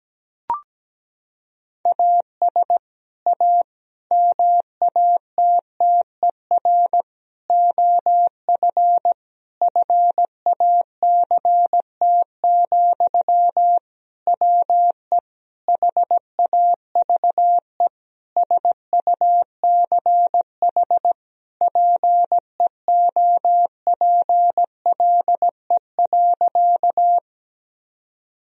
Morse code, heard as a keyed tone, about 17 wpm, 700 Hz. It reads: AS A MATTER OF FACT, WE HAVE SUCH PEOPLE.
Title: Sentences from Top 1000 Words - Review Part 3 - 17wpm